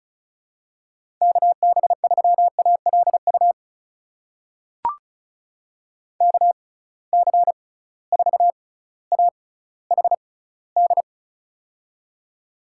Morse code, heard as a keyed tone, about 36 wpm, spaced as for 18 wpm, 700 Hz. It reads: KB3ALU KC4AHD
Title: USA Call Signs - 6x Character Spacing - 35wpm